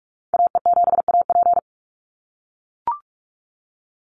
Morse code, heard as a keyed tone, about 44 wpm, 700 Hz. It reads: AE7AP
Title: NA SOTA Chaser Callsigns 45wpm